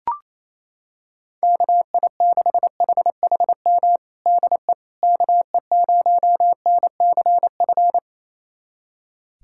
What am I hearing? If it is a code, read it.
Morse code, heard as a keyed tone, about 28 wpm, 700 Hz. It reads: KI6HHM DE KE0NCF
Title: USA Callsign DE Callsign 28wpm